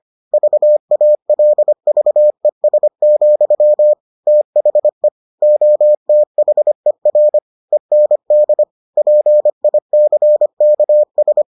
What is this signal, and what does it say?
Morse code, heard as a keyed tone, about 25 wpm, 600 Hz. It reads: VALVES, THE OTHER END PICKS